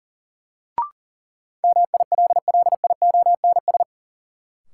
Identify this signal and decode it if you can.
Morse code, heard as a keyed tone, about 40 wpm, 700 Hz. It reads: MILLIONS
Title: Sets of 5 Words - New Word Drill - 40wpm